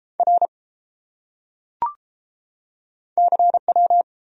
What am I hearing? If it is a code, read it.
Morse code, heard as a keyed tone, about 30 wpm, 700 Hz. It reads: R CW